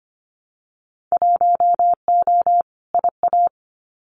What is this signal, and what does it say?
Morse code, heard as a keyed tone, about 25 wpm, 700 Hz. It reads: 1O IA
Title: Winter Field Day Exchanges 25wpm